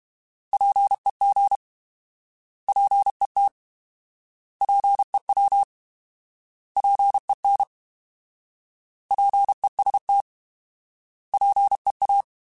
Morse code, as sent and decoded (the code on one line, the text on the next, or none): .--. --. / .--. - / .--. .-- / .--. -. / .--. ... - / .--. .-
PEG PET PEW PEN PEST PEA